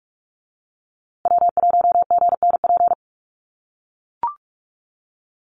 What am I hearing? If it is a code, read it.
Morse code, heard as a keyed tone, about 44 wpm, 690 Hz. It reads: W1GNP
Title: USA Call Signs Repeated 3x 45wpm